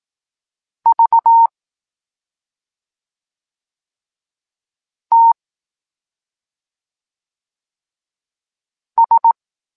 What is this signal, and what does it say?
Morse code, heard as a keyed tone, about 18 wpm, 920 Hz. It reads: V T S
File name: Tekst-26 1